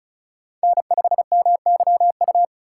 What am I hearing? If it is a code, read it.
Morse code, heard as a keyed tone, about 36 wpm, 700 Hz. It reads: N5MYU
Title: USA Call Signs - 5 Characters - 2x Character Spacing - 35wpm